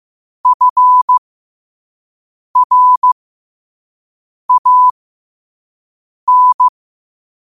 Morse code, heard as a keyed tone, about 15 wpm, spaced as for 5 wpm, 1000 Hz. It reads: FRAN
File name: qotdfiles / 5wpm